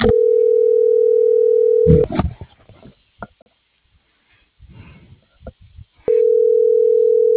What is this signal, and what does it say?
Ringback tone (North America), 440 + 480 Hz.
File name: CENTRAL MAERSK / out adolfo.msk